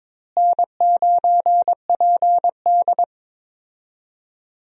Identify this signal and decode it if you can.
Morse code, heard as a keyed tone, about 22 wpm, 700 Hz. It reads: N9PD
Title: USA Call Signs - 4 Characters - Repeated 3x - 22wpm